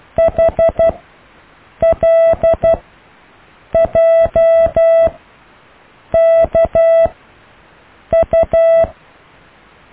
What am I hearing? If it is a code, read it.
Morse code, heard as a keyed tone, about 12 wpm, spaced as for 6 wpm, 660 Hz. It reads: HLJKU
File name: cw / IJLUM+CDRO+EKTPV+BFHNW